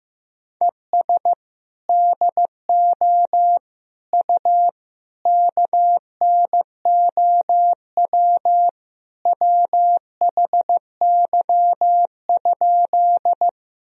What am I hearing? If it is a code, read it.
Morse code, heard as a keyed tone, about 16 wpm, 700 Hz. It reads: ES DO U KNOW WHY?
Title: Sentences from Top 200 Words and Abbreviations 15wpm